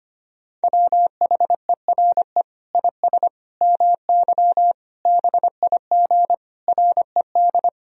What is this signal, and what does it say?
Morse code, heard as a keyed tone, about 25 wpm, 700 Hz. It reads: WHERE IS MY BIG RED
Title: Sentences from Top 1000 Words - Review Part 2 - 25wpm